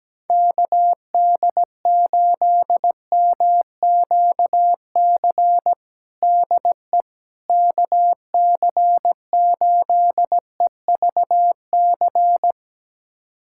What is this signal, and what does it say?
Morse code, heard as a keyed tone, about 17 wpm, 700 Hz. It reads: KD8MQC DE KC8EVC
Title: USA Callsign DE Callsign 17wpm